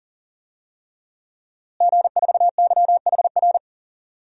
Morse code, heard as a keyed tone, about 40 wpm, 700 Hz. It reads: G4YHR
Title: DXCC-100 Call Signs - 5 Characters - Repeated 3x - 40wpm